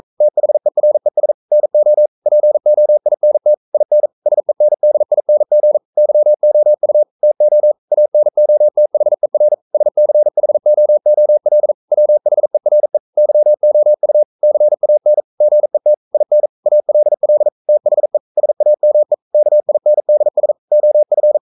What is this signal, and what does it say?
Morse code, heard as a keyed tone, about 42 wpm, 600 Hz. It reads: THERES NO POINT IN SENDING YOU TO ANOTHER SCHOOL WHERE YOU CAN GET IN ALL THE SAME KINDS OF
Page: Romanen Never af Ken Follet på engelsk.